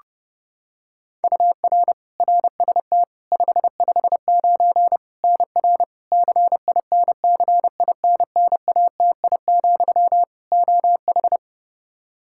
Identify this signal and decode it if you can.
Morse code, heard as a keyed tone, about 32 wpm, 700 Hz. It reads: UR RST 559 NR CINCINNATI, OH
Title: K1USN SST QSO Practice 30wpm